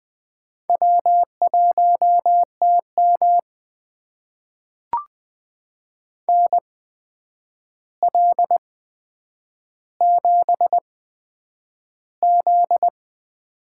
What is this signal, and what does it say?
Morse code, heard as a keyed tone, about 20 wpm, spaced as for 9 wpm, 700 Hz. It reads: W1TM NL7Z
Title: USA Call Signs - 4 Characters - 8x Character Spacing - 20wpm